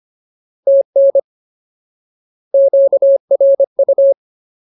Morse code, heard as a keyed tone, about 25 wpm, 550 Hz. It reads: TN QRU
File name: Ultimate Word List / 25WPM Ultimate